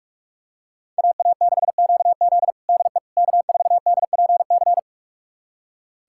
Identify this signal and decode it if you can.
Morse code, heard as a keyed tone, about 46 wpm, 700 Hz. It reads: AA6QZ DE K4DPC
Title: USA Callsign DE Callsign - 5 Characters - 45wpm